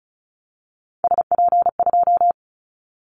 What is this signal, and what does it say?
Morse code, heard as a keyed tone, about 36 wpm, 700 Hz. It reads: SP2
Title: Callsign Prefixes for Top 25 DXCC - 35wpm